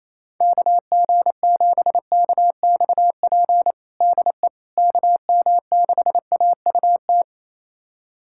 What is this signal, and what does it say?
Morse code, heard as a keyed tone, about 27 wpm, 700 Hz. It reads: KG7KXP DE KM6AUT
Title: USA Callsign DE Callsign - 6 Characters - 28wpm